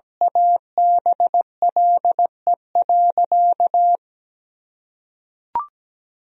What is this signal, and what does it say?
Morse code, heard as a keyed tone, about 17 wpm, 700 Hz. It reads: ABLE.
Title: Sentences from Top 1000 Words - Review Part 3 - 17wpm